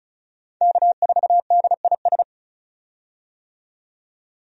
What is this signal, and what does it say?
Morse code, heard as a keyed tone, about 34 wpm, 700 Hz. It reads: K4DIS
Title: USA Call Signs - 5 Characters - 35wpm